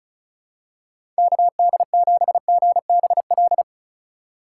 Morse code, heard as a keyed tone, about 36 wpm, 700 Hz. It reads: KD7GBL
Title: USA Call Signs - 6 Characters - 2x Character Spacing - 35wpm